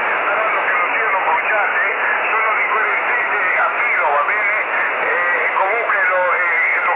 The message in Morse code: none